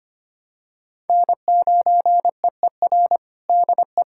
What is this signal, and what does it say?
Morse code, heard as a keyed tone, about 25 wpm, 700 Hz. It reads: N9EER DE
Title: USA Callsign DE Callsign - 5 Characters - 25wpm